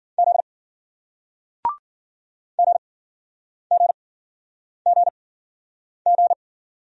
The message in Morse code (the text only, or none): Z GGGG